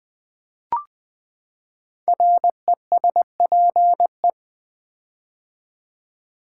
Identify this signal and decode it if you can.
Morse code, heard as a keyed tone, about 20 wpm, 700 Hz. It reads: RESPE
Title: Build A Word - 5 Letters At A Time - 12-Letter Words - 20wpm